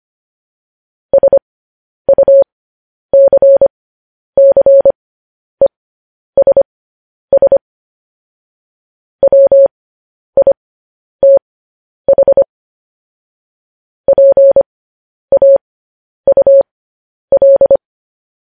Morse code, heard as a keyed tone, about 25 wpm, spaced as for 10 wpm, 550 Hz. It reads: SUCCESS WITH PAUL
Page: news headlines in morse code at 10 wpm